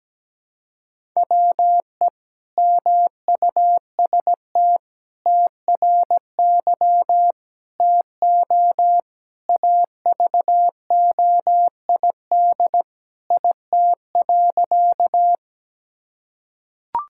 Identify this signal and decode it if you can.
Morse code, heard as a keyed tone, about 17 wpm, 700 Hz. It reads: WE MUST TRY TO AVOID IT.
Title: Sentences from Top 1000 Words - Review Part 6 - 17wpm